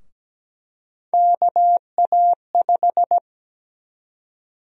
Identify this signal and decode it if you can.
Morse code, heard as a keyed tone, about 17 wpm, 700 Hz. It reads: KA5
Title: USA Call Sign Prefixes - 17wpm